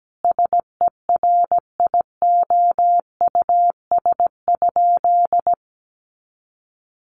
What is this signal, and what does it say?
Morse code, heard as a keyed tone, about 17 wpm, 700 Hz. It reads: SERIOUS?
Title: Sentences from Top 1100 Words 17wpm